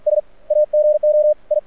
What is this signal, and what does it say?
Morse code, heard as a keyed tone, about 48 wpm, spaced as for 34 wpm, 600 Hz, with radio noise. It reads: SAGO I
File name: archived / CRSnet